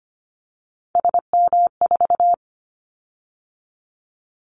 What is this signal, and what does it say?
Morse code, heard as a keyed tone, about 25 wpm, 700 Hz. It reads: SM4
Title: Callsign Prefixes for Top 25 DXCC - 25wpm